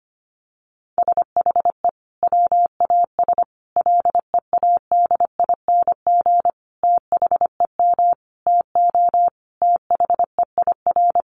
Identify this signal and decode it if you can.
Morse code, heard as a keyed tone, about 25 wpm, 700 Hz. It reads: SHE WAS LEADING THEM TO THEIR
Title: Sentences from Top 1100 Words 25wpm